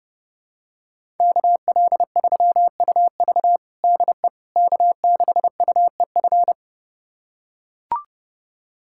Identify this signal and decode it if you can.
Morse code, heard as a keyed tone, about 30 wpm, 700 Hz. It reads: KL3UV DE K6UEF
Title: USA Callsign DE Callsign - 5 Characters - 30wpm